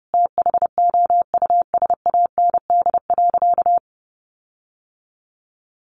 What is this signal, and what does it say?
Morse code, heard as a keyed tone, about 30 wpm, 700 Hz. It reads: THOUSAND.
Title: Sentences from Top 1300 Words 30wpm